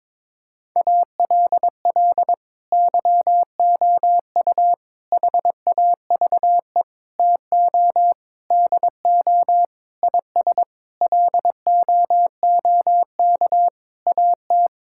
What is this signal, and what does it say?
Morse code, heard as a keyed tone, about 22 wpm, 700 Hz. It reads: ALL YOU HAVE TO DO IS LOOK AT
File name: Sentences with 9-Word Phrases - 22wpm